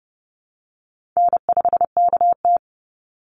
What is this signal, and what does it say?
Morse code, heard as a keyed tone, about 29 wpm, 700 Hz. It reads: N5KT
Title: Easy Call Signs Repeated 3x - 30wpm